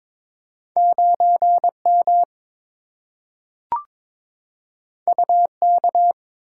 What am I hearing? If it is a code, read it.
Morse code, heard as a keyed tone, about 23 wpm, 700 Hz. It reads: 9M UK
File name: practice / 22wpm / Two Letter-Number Combinations - 22wpm